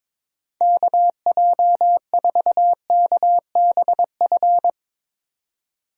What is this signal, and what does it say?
Morse code, heard as a keyed tone, about 22 wpm, 700 Hz. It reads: KJ4KBF
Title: USA Call Signs - 6 Characters - 22wpm